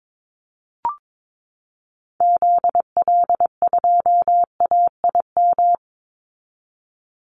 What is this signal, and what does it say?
Morse code, heard as a keyed tone, about 22 wpm, 700 Hz. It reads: ZL2AIM
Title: Call Signs Repeated 3x - 22wpm